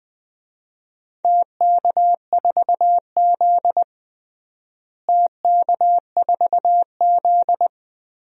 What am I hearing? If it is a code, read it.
Morse code, heard as a keyed tone, about 21 wpm, 700 Hz. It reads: TK4Z TK4Z
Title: DXCC-100 Call Signs - 4 Characters - Repeated 2x - 20wpm